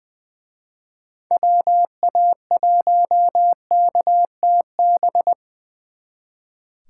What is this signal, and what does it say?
Morse code, heard as a keyed tone, about 24 wpm, 700 Hz. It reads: WA1KTB
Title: Speed Racing - USA 2x3 Call Signs 20wpm